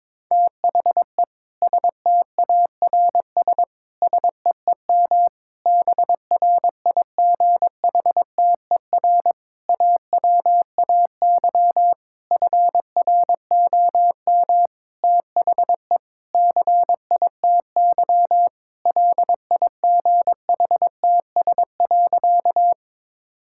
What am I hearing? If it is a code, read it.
Morse code, heard as a keyed tone, about 22 wpm, 700 Hz. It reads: THE STARS SEEM BRIGHTER AWAY FROM THE CITY LIGHTS.
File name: Difficult Sentences with Hints - 22wpm